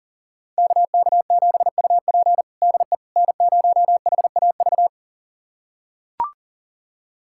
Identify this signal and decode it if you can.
Morse code, heard as a keyed tone, about 40 wpm, 700 Hz. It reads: KK7UP DE N0HAV
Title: USA Callsign DE Callsign - 5 Characters - 40wpm